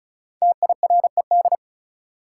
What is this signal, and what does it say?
Morse code, heard as a keyed tone, about 36 wpm, 700 Hz. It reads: TIRED
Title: Five Letter Words - 35wpm